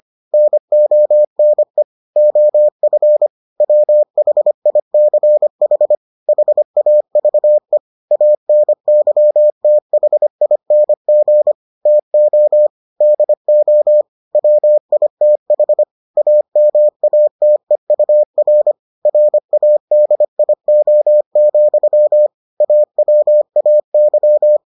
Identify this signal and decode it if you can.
Morse code, heard as a keyed tone, about 25 wpm, 600 Hz. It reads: NONE OF WHICH HAVE ANYTHING TO DO WITH AMATEUR RADIO, AWAY